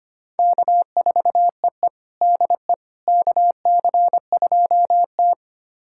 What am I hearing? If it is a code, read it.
Morse code, heard as a keyed tone, about 25 wpm, 700 Hz. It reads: K4EE DE KC2T
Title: USA Callsign DE Callsign - 4 Characters - 25wpm